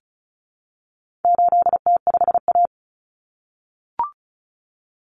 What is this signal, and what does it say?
Morse code, heard as a keyed tone, about 34 wpm, 700 Hz. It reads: 8T5A
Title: Common Cut Numbers - 4-Digit 35wpm